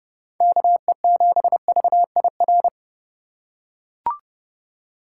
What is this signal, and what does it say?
Morse code, heard as a keyed tone, about 30 wpm, 700 Hz. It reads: KE7VIR